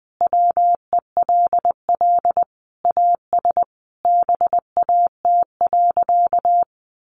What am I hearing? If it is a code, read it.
Morse code, heard as a keyed tone, about 20 wpm, 700 Hz. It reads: WELL AS BAT.
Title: Sentences from Top 1500 Words 20wpm